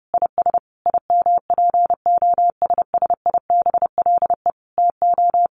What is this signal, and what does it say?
Morse code, heard as a keyed tone, about 30 wpm, 700 Hz. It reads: IS IMPOSSIBLE TO
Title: Sentences from Top 1200 Words 30wpm